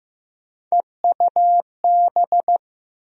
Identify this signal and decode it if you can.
Morse code, heard as a keyed tone, about 15 wpm, 700 Hz. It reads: EUB
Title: USA Call Sign Postfixes - 3 Letters 15wpm